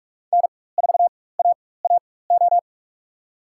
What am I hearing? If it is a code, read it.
Morse code, heard as a keyed tone, about 44 wpm, spaced as for 21 wpm, 700 Hz. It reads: N4AAO
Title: USA Call Signs - 5 Characters - 4x Character Spacing - 45wpm